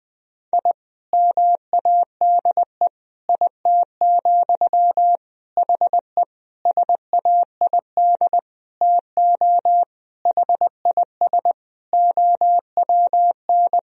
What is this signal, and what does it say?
Morse code, heard as a keyed tone, about 20 wpm, 700 Hz. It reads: I MADE IT, HE SAID TO HIS OWN
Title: Sentences from Top 1000 Words - Review Part 1 - 20wpm